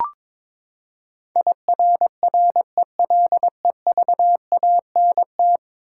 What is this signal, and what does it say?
Morse code, heard as a keyed tone, about 22 wpm, 700 Hz. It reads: IRRELEVANT
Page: morse-code-ninja-practice-22wpm